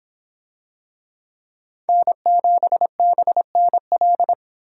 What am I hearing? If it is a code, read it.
Morse code, heard as a keyed tone, about 25 wpm, 700 Hz. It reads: N7BNL